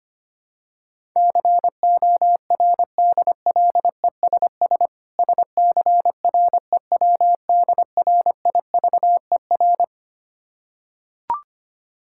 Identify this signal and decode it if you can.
Morse code, heard as a keyed tone, about 25 wpm, 690 Hz. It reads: CORDLESS SCREWDRIVER
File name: practice / 25wpm / Difficult Words with Hints - 25wpm